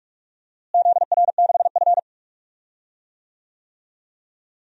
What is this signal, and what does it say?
Morse code, heard as a keyed tone, about 46 wpm, 700 Hz. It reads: ZR6F
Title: DXCC-100 Call Signs - 4 Characters - Repeated 3x - 45wpm